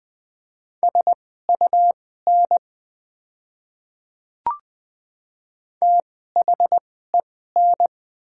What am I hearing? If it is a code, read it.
Morse code, heard as a keyed tone, about 20 wpm, spaced as for 14 wpm, 690 Hz. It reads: SUN THEN
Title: 2x Character Spacing - Top 300 Words - 20wpm